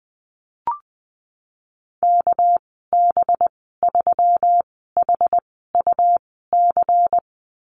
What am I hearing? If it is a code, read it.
Morse code, heard as a keyed tone, about 20 wpm, spaced as for 14 wpm, 700 Hz. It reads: KB3HUC
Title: 207 Lesson 33g - Letter j - Call Signs 20wpm